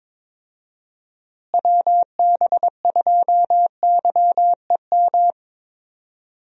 Morse code, heard as a keyed tone, about 22 wpm, 700 Hz. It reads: WB2YEM
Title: USA Call Signs - 6 Characters - Repeated 3x - 22wpm